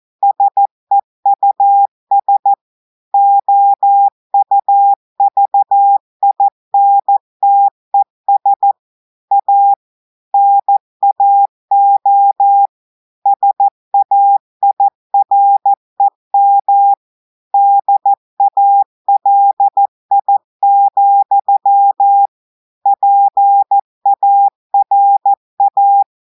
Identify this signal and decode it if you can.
Morse code, heard as a keyed tone, about 14 wpm, 800 Hz. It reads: SEUS OUVINTES A NAO SAIREM DALI, PARA
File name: audios / portugais / portugais texte en clair 2 vitesse 14 mots minutes